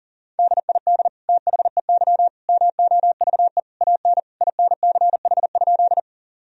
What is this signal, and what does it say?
Morse code, heard as a keyed tone, about 40 wpm, 700 Hz. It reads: DID THEY MOVE AN INCH?
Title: Sentences from Top 1000 Words - Review Part 3 - 40wpm